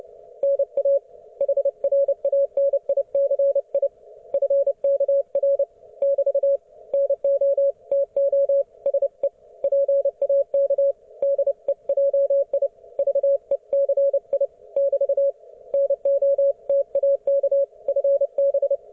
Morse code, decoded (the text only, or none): NA HRANICI FKR <BT> NO TO SE PAK DEJI VECI <BT> NO TAK FB